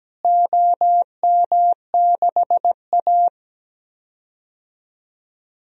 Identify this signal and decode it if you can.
Morse code, heard as a keyed tone, about 17 wpm, 700 Hz. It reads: OM6A
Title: DXCC-100 Call Signs - 4 Characters - Repeated 3x - 17wpm